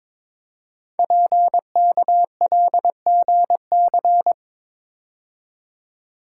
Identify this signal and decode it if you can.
Morse code, heard as a keyed tone, about 22 wpm, 700 Hz. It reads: PKLGC
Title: Code Groups - 5 Letter:Numbers - 22wpm